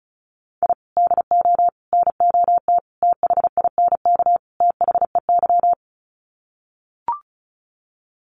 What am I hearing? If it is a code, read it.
Morse code, heard as a keyed tone, about 34 wpm, 700 Hz. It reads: I DO NOT THINK THEY
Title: Sets of 5 Words 35wpm